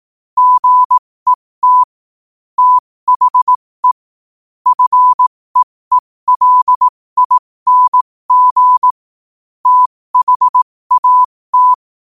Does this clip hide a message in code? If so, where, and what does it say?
Yes: Morse code from 0.4 to 11.7 s, heard as a keyed tone, about 18 wpm, spaced as for 15 wpm, 1000 Hz. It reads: GET THE FEELING THAT